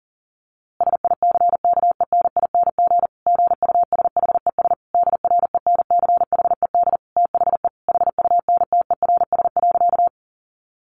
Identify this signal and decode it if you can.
Morse code, heard as a keyed tone, about 40 wpm, 700 Hz. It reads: SICKENING GUSHES DRENCHED THE HUNTERS.
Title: A Sound Of Thunder 40wpm